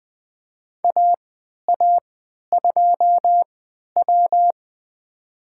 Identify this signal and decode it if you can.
Morse code, heard as a keyed tone, about 20 wpm, spaced as for 11 wpm, 700 Hz. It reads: AA2W
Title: USA Call Signs - 3x Character Spacing - 20wpm